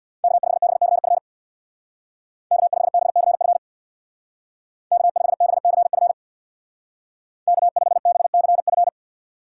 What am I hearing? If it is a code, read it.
Morse code, heard as a keyed tone, about 59 wpm, 700 Hz. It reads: K5BCF K5BCF K5BCF K5BCF
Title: Speed Racing - USA Call Signs 50wpm